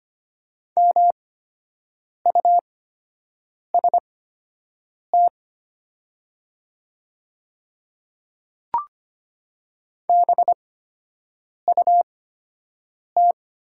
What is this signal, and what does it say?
Morse code, heard as a keyed tone, about 25 wpm, spaced as for 7 wpm, 700 Hz. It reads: MUST BUT
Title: 8x Character Spacing - Top 300 Words - 25wpm